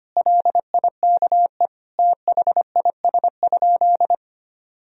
Morse code, heard as a keyed tone, about 25 wpm, 700 Hz. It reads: LIKE THIS?
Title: Sentences from Top 1000 Words - Review Part 2 - 25wpm